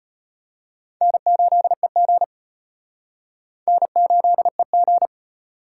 Morse code, heard as a keyed tone, about 31 wpm, 700 Hz. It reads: N8EG N8EG
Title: Speed Racing - USA 1x2 Call Signs 25wpm